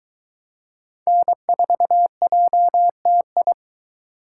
Morse code, heard as a keyed tone, about 22 wpm, 700 Hz. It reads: N4JTI